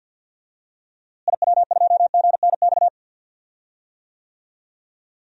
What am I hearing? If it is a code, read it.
Morse code, heard as a keyed tone, about 47 wpm, 700 Hz. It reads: IW2GNX
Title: DXCC-100 Call Signs - 6 Characters - Repeated 3x - 50wpm